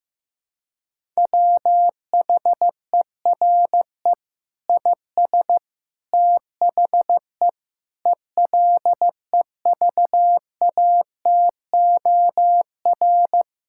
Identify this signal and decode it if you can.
Morse code, heard as a keyed tone, about 15 wpm, 700 Hz. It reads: WHERE IS THE ELEVATOR
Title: Sentences from Top 1600 Words 15wpm